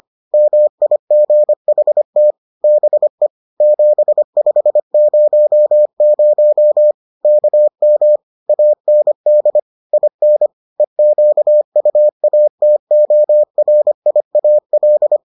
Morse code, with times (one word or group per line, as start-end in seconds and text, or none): 0.3-2.3 MIGHT
2.6-3.3 BE
3.6-6.9 7500
7.2-8.2 KM
8.5-9.6 AND
9.9-10.5 IN
10.8-15.2 EQUATORIAL